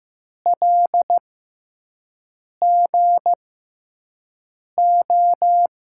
Morse code, heard as a keyed tone, about 15 wpm, spaced as for 5 wpm, 700 Hz. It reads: LGO
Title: USA Call Signs - 6 Characters - 6x Character Spacing - 15wpm